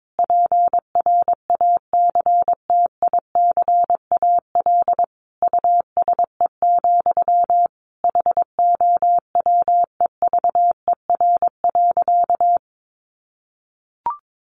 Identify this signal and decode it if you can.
Morse code, heard as a keyed tone, about 22 wpm, 700 Hz. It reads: PRACTICAL USE, HOWEVER.
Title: Sentences from Top 1400 Words 22wpm